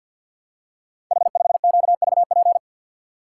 Morse code, heard as a keyed tone, about 53 wpm, 700 Hz. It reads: H5QVP
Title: Code Groups - 5 Letter:Numbers - 50wpm